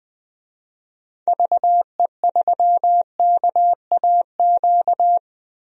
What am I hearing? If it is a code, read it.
Morse code, heard as a keyed tone, about 20 wpm, 700 Hz. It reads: VE3KAQ
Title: DXCC-100 Call Signs - 6 Characters - Repeated 2x - 20wpm